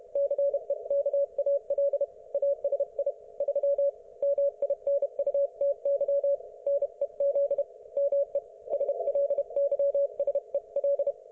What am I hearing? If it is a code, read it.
Morse code, heard as a keyed tone, about 31 wpm, 560 Hz. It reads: CEKAL ASI 3 MINUTY NEZ ME SLYSEL